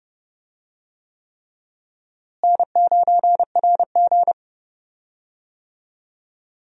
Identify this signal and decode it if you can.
Morse code, heard as a keyed tone, about 29 wpm, 700 Hz. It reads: N9RG
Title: USA Call Signs - 4 Characters - Repeated 3x - 30wpm